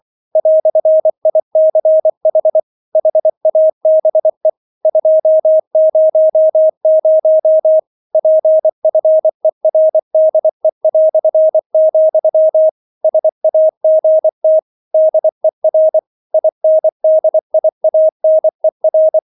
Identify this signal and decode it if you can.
Morse code, heard as a keyed tone, about 24 wpm, 630 Hz. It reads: "ICH HABE 200 PFERDE", SAGT DER INDIANER